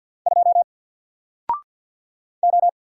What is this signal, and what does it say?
Morse code, heard as a keyed tone, about 48 wpm, 700 Hz. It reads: 2 O
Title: Single Letter-Number Alt 50wpm